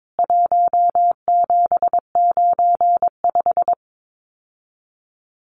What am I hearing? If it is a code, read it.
Morse code, heard as a keyed tone, about 22 wpm, 700 Hz. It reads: 1795
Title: CWT Exchange 22wpm